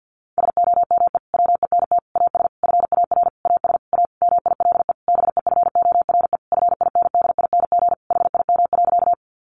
Morse code, heard as a keyed tone, about 50 wpm, 700 Hz. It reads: SOME WENT AS FAR AS A MILE BEFORE FINDING HIM.
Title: Sentences from Top 1100 Words 50wpm